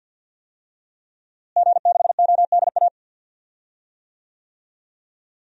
Morse code, heard as a keyed tone, about 52 wpm, 700 Hz. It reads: G6ODA
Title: DXCC-100 Call Signs - 5 Characters - Repeated 2x - 50wpm